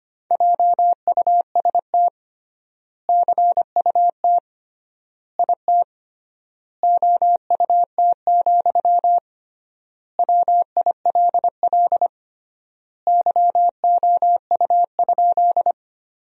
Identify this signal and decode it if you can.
Morse code, heard as a keyed tone, about 25 wpm, 700 Hz. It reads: JUST CUT IT OUT, WILL YOU?
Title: 3x Word-Spacing - Sentences from Top 300 Words - 25wpm